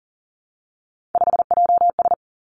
Morse code, heard as a keyed tone, about 41 wpm, 700 Hz. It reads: 5JS
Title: Three Challenging Letter-Number Combinations 40wpm